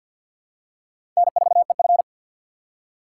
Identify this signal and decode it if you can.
Morse code, heard as a keyed tone, about 53 wpm, 700 Hz. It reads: N4EF
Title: USA Call Signs - 4 Characters - 3x Character Spacing - 50wpm